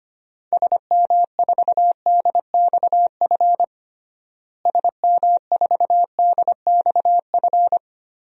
Morse code, heard as a keyed tone, about 25 wpm, 700 Hz. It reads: SM4DXF SM4DXF
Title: DXCC-100 Call Signs - 6 Characters - Repeated 2x - 25wpm